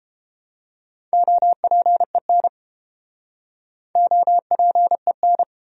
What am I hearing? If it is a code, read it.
Morse code, heard as a keyed tone, about 30 wpm, 700 Hz. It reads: OPEN OPEN